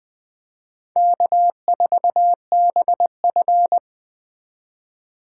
Morse code, heard as a keyed tone, about 20 wpm, 700 Hz. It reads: K4BF
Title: Speed Racing - USA 1x2 Call Signs 17wpm